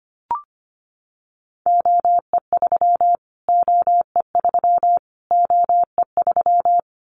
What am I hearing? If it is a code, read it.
Morse code, heard as a keyed tone, about 25 wpm, 700 Hz. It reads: OE3 OE3 OE3
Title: Callsign Prefixes for Top 25 DXCC - Repeated 3x - 25wpm